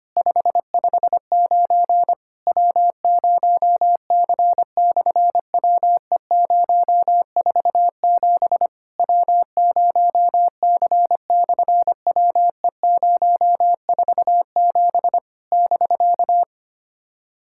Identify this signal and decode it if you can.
Morse code, heard as a keyed tone, about 25 wpm, 700 Hz. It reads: 559 W0C/WE047 W0C/WE047 <BK>
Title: SOTA QSO Practice 25wpm